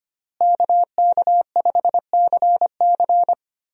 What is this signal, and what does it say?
Morse code, heard as a keyed tone, about 25 wpm, 700 Hz. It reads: KK5CC
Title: USA Call Signs - 5 Characters - Repeated 3x - 25wpm